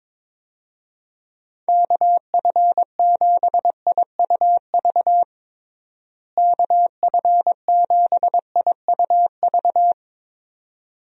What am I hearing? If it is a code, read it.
Morse code, heard as a keyed tone, about 22 wpm, 700 Hz. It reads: KF7IUV KF7IUV
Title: USA Call Signs - 6 Characters - Repeated 2x - 22wpm